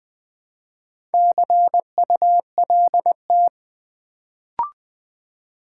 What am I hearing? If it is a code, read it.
Morse code, heard as a keyed tone, about 20 wpm, 700 Hz. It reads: CULT
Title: Build A Word - 1 Letter At A Time - 4-Letter Words - 20wpm